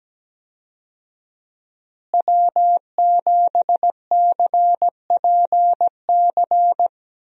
Morse code, heard as a keyed tone, about 17 wpm, 700 Hz. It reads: W7CPC
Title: USA Call Signs - 5 Characters - Repeated 2x - 17wpm